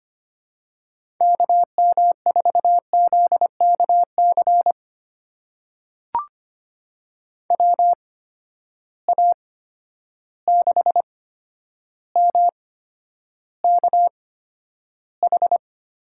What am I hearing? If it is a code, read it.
Morse code, heard as a keyed tone, about 25 wpm, spaced as for 11 wpm, 700 Hz. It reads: KM4ZKC WA6MKH
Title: USA Call Signs - 6 Characters - 8x Character Spacing - 25wpm